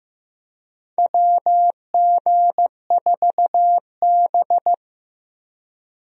Morse code, heard as a keyed tone, about 15 wpm, 700 Hz. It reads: WG4B
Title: USA Call Signs - 4 Characters - Repeated 2x - 15wpm